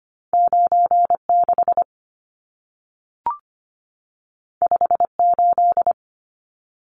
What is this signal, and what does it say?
Morse code, heard as a keyed tone, about 25 wpm, 700 Hz. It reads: 96 58
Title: Numbers - 2-Digit 25wpm